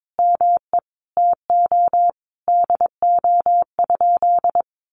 Morse code, heard as a keyed tone, about 22 wpm, 700 Hz. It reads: ME TO DO?
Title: Sentences from Top 200 Words 22wpm